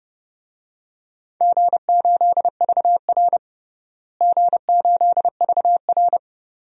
Morse code, heard as a keyed tone, about 32 wpm, 700 Hz. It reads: G8VR G8VR
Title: DXCC-100 Call Signs - 4 Characters - Repeated 2x - 30wpm